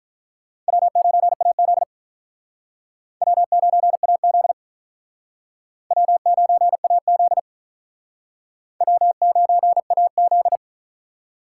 Morse code, heard as a keyed tone, about 45 wpm, 700 Hz. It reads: W9AZ W9AZ W9AZ W9AZ